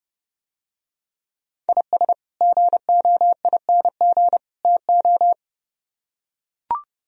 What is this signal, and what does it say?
Morse code, heard as a keyed tone, about 29 wpm, 700 Hz. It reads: IS GOING TO